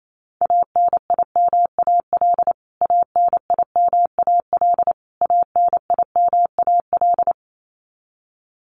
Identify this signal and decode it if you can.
Morse code, heard as a keyed tone, about 27 wpm, 700 Hz. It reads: ANIMAL ANIMAL ANIMAL
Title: Top 200 Words Repeated 28wpm